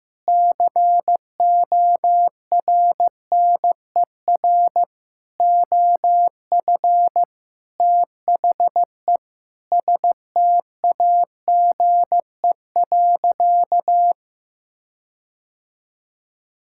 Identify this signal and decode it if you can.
Morse code, heard as a keyed tone, about 15 wpm, 700 Hz. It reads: CORNER OF THE STAGE.
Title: Sentences from Top 1600 Words 15wpm